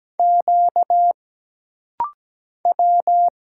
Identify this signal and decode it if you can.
Morse code, heard as a keyed tone, about 17 wpm, 700 Hz. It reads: Q W
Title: Single Letters - Rapid-Fire 17wpm